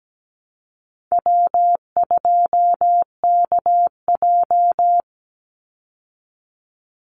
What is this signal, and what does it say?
Morse code, heard as a keyed tone, about 17 wpm, 700 Hz. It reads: W2KJ
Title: Easy Call Signs Repeated 3x - 17wpm